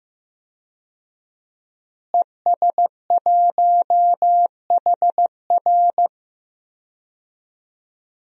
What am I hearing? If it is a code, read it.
Morse code, heard as a keyed tone, about 16 wpm, 700 Hz. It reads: ES1HR
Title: DXCC-100 Call Signs - 5 Characters - Repeated 3x - 15wpm